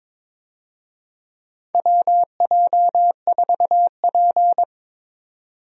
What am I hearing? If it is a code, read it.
Morse code, heard as a keyed tone, about 22 wpm, 700 Hz. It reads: WJ4P
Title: USA Call Signs - 4 Characters - 22wpm